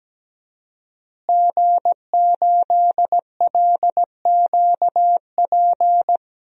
Morse code, heard as a keyed tone, about 17 wpm, 700 Hz. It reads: G8LQP
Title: DXCC-100 Call Signs - 5 Characters - Repeated 2x - 17wpm